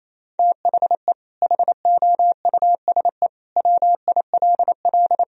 Morse code, heard as a keyed tone, about 27 wpm, 700 Hz. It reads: THE HOUSE WILL
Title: Sentences from Top 1000 Words - Review Part 2 - 28wpm